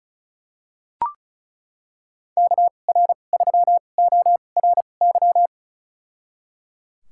Morse code, heard as a keyed tone, about 36 wpm, spaced as for 25 wpm, 700 Hz. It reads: KR3ORY
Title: USA Call Signs - 6 Characters - 2x Character Spacing - 35wpm